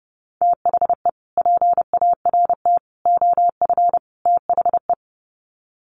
Morse code, heard as a keyed tone, about 30 wpm, 700 Hz. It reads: THE PART OF THE
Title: Sets of 4 Words 30wpm